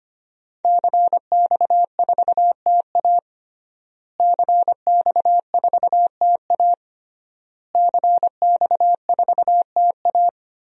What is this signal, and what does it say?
Morse code, heard as a keyed tone, about 25 wpm, 700 Hz. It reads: CX4TA CX4TA CX4TA
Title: DXCC-100 Call Signs - 5 Characters - Repeated 3x - 25wpm